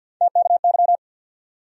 Morse code, heard as a keyed tone, about 53 wpm, 700 Hz. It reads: TKY
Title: USA Call Sign Postfixes - 3 Letters 50wpm